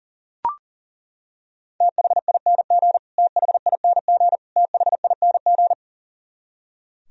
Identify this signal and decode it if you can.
Morse code, heard as a keyed tone, about 40 wpm, 700 Hz. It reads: THING THING THING
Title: Top 100 Words Repeated 40wpm